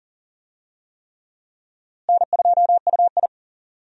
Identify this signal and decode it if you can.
Morse code, heard as a keyed tone, about 40 wpm, 700 Hz. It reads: N2UI
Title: USA Call Signs - 4 Characters - Repeated 3x - 40wpm